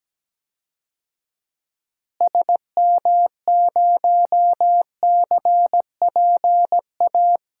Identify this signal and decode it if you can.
Morse code, heard as a keyed tone, about 17 wpm, 700 Hz. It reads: SM0CPA
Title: DXCC-100 Call Signs - 6 Characters - Repeated 3x - 17wpm